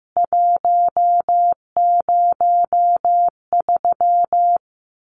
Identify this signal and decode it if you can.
Morse code, heard as a keyed tone, about 15 wpm, 700 Hz. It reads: 103
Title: CWT Exchange 15wpm